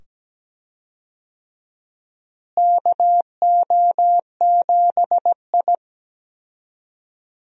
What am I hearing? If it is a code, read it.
Morse code, heard as a keyed tone, about 17 wpm, 700 Hz. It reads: KO7I
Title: USA Call Signs - 4 Characters - Repeated 2x - 17wpm